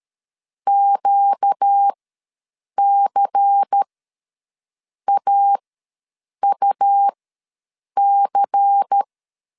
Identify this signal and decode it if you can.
Morse code, heard as a keyed tone, about 13 wpm, spaced as for 7 wpm, 800 Hz. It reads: QCAUC
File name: LEZIONI / 5 LBVC / 5 LBVC ws 12wpm 3159